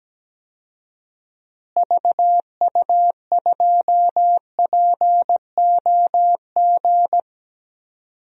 Morse code, heard as a keyed tone, about 17 wpm, 700 Hz. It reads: VU2POG
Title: DXCC-100 Call Signs - 6 Characters - Repeated 3x - 17wpm